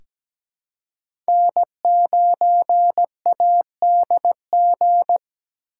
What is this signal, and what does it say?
Morse code, heard as a keyed tone, about 17 wpm, 700 Hz. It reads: N9ADG
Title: K1USN SST QSO Practice - 17wpm